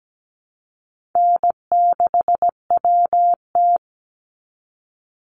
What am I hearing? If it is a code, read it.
Morse code, heard as a keyed tone, about 17 wpm, 700 Hz. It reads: N6WT
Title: NA SOTA Chaser Callsigns 17wpm